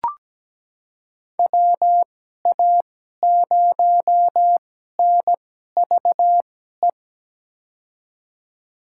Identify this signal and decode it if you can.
Morse code, heard as a keyed tone, about 17 wpm, spaced as for 12 wpm, 700 Hz. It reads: WA0NVE
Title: USA Call Signs - 6 Characters - 2x Character Spacing - 17wpm